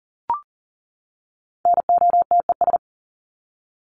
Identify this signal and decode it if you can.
Morse code, heard as a keyed tone, about 40 wpm, 700 Hz. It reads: NOTES
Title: ES:S Suffix - Top 100 Words with Base Words 40wpm